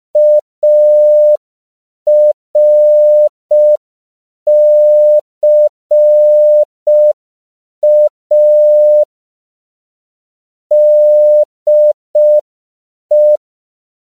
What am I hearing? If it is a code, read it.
Morse code, heard as a keyed tone, about 5 wpm, 600 Hz. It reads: ARCA DE